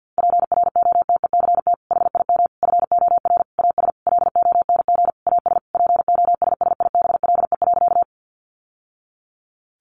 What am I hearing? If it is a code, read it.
Morse code, heard as a keyed tone, about 50 wpm, 700 Hz. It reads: PROTECT HIM FOR AS LONG AS POSSIBLE.
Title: Sentences from Top 900 Words 50wpm